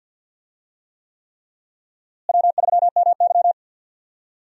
Morse code, heard as a keyed tone, about 47 wpm, 700 Hz. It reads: W3MY
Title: USA Call Signs - 4 Characters - Repeated 3x - 50wpm